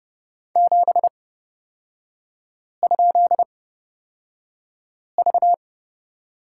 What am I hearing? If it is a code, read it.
Morse code, heard as a keyed tone, about 30 wpm, spaced as for 5 wpm, 700 Hz. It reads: 7?V
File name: Lesson 36e - BT - Three Characters 30wpm